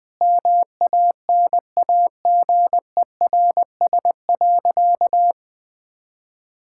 Morse code, heard as a keyed tone, about 20 wpm, 700 Hz. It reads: MANAGERS.
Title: Sentences with 6-Word Phrases - Spoken Beginning - 20wpm